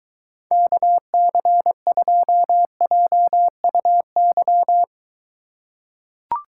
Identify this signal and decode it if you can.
Morse code, heard as a keyed tone, about 23 wpm, 700 Hz. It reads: KC2JUY